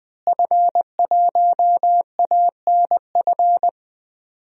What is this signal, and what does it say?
Morse code, heard as a keyed tone, about 20 wpm, 700 Hz. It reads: F1ANF
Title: DXCC-100 Call Signs - 5 Characters - Repeated 3x - 20wpm